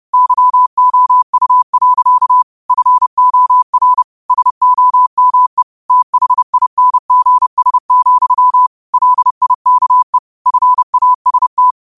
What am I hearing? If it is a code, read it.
Morse code, heard as a keyed tone, about 30 wpm, 1000 Hz. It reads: YOU. FOR SOME THINGS, LIKE FAST
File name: qotdfiles / 30wpm